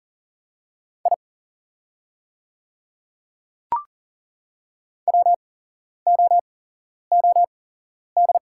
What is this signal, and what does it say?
Morse code, heard as a keyed tone, about 40 wpm, spaced as for 11 wpm, 700 Hz. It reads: I WOOD
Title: 8x Character Spacing - Top 500 Words - 40wpm